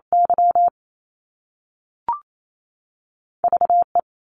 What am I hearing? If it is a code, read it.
Morse code, heard as a keyed tone, about 28 wpm, 700 Hz. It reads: Y V E
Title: Top Two Letter Combinations in Words 28wpm